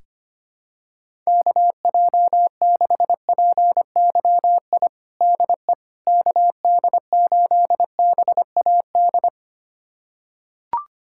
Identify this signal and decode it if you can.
Morse code, heard as a keyed tone, about 25 wpm, 700 Hz. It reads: KJ6PYI DE KD8BAD
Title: USA Callsign DE Callsign - 6 Characters - 25wpm